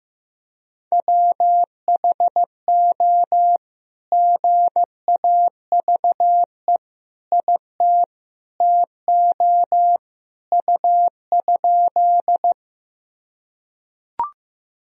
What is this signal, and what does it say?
Morse code, heard as a keyed tone, about 15 wpm, 700 Hz. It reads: WHO GAVE IT TO U?
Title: Sentences from Top 500 Words and Abbreviations 15wpm